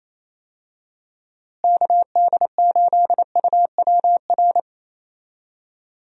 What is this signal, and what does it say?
Morse code, heard as a keyed tone, about 27 wpm, 700 Hz. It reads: KD8UWR
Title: USA Call Signs - 6 Characters - Repeated 2x - 28wpm